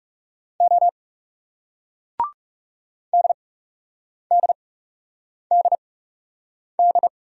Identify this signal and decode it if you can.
Morse code, heard as a keyed tone, about 40 wpm, spaced as for 8 wpm, 700 Hz. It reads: O DDDD